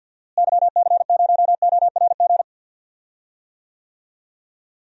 Morse code, heard as a keyed tone, about 52 wpm, 700 Hz. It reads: YC0ORG
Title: DXCC-100 Call Signs - 6 Characters - Repeated 2x - 50wpm